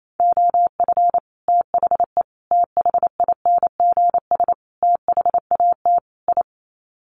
Morse code, heard as a keyed tone, about 28 wpm, 700 Hz. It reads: OF THE THINGS THAT I
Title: Sets of 5 Words Encore 28wpm